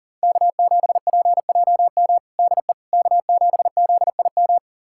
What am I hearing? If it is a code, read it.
Morse code, heard as a keyed tone, about 40 wpm, 700 Hz. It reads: K7PJM DE K7ZIM
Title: USA Callsign DE Callsign - 5 Characters - 40wpm